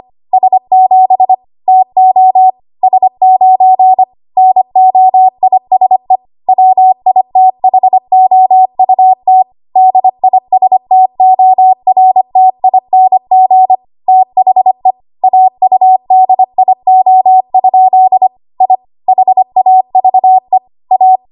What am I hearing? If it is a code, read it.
Morse code, heard as a keyed tone, about 25 wpm, 750 Hz. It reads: S7 TO S9 NOISE WITHOUT DISTORTING THE AUDIO? I HAVE A
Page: W1AW Web Code Practice Files